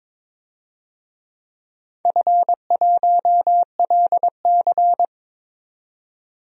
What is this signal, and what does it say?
Morse code, heard as a keyed tone, about 22 wpm, 700 Hz. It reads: F1LC
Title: DXCC-100 Call Signs - 4 Characters - 22wpm